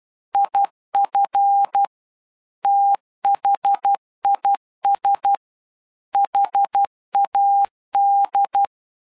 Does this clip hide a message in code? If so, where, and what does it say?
Yes: Morse code from 0.3 to 8.7 s, heard as a keyed tone, about 12 wpm, 800 Hz. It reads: IF THIS HAD